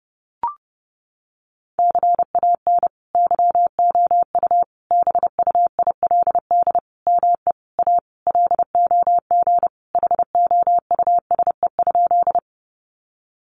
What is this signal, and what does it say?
Morse code, heard as a keyed tone, about 30 wpm, 700 Hz. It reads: CAN YOU BUILD ME A LOG HOUSE?
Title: Sentences from Top 1400 Words 30wpm